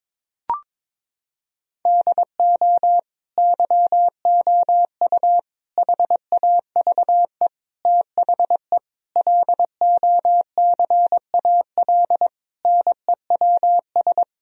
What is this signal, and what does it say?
Morse code, heard as a keyed tone, about 22 wpm, 700 Hz. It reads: DO YOU HAVE THE LOCAL NEWS
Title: Sentences from Top 1000 Words - Review Part 3 - 22wpm